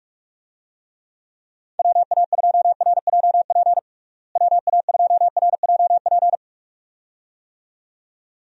Morse code, heard as a keyed tone, about 44 wpm, 700 Hz. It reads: WA2RJP WA2RJP
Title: USA Call Signs - 6 Characters - Repeated 2x - 45wpm